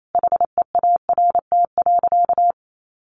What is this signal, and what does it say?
Morse code, heard as a keyed tone, about 28 wpm, 700 Hz. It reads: HEART.
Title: Sentences from Top 1400 Words 28wpm